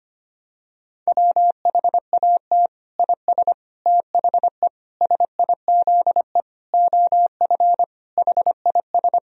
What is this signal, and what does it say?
Morse code, heard as a keyed tone, about 25 wpm, 700 Hz. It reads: WHAT IS THE SIZE OF HIS
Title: Sentences from Top 1000 Words - Review Part 2 - 25wpm